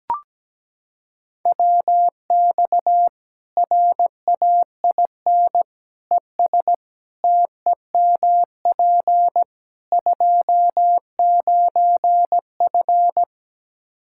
Morse code, heard as a keyed tone, about 17 wpm, 700 Hz. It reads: WX RAIN ES TEMP 29F
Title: Practice QSO - Exchange 2 - Info - 17wpm